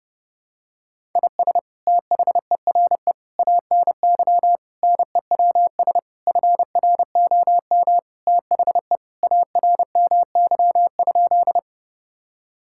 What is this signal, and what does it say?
Morse code, heard as a keyed tone, about 30 wpm, 700 Hz. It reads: IS THERE ANY NEWS FROM THE ARMY?
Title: Sentences from Top 1000 Words - Review Part 5 - 30wpm